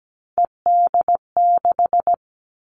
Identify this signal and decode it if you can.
Morse code, heard as a keyed tone, about 17 wpm, 700 Hz. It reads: ED6
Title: Callsign Prefixes for Top 100 DXCC - 17wpm